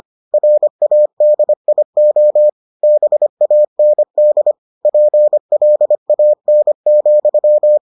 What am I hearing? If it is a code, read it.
Morse code, heard as a keyed tone, about 25 wpm, 600 Hz. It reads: RADIO BAND PLAN,